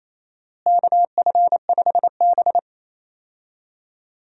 Morse code, heard as a keyed tone, about 28 wpm, 700 Hz. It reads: KF5B
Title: USA Call Signs - 4 Characters - Repeated 2x - 28wpm